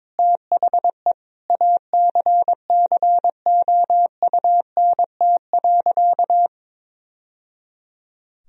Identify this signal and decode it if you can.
Morse code, heard as a keyed tone, about 22 wpm, 700 Hz. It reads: THE ACCOUNT.
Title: Sentences from Top 1000 Words - Review Part 4 - 22wpm